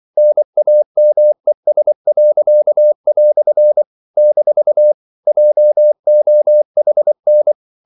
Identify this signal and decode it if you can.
Morse code, heard as a keyed tone, about 24 wpm, 600 Hz. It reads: NAMES." - JOHN
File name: CW / Practice / EnglishQuotations0324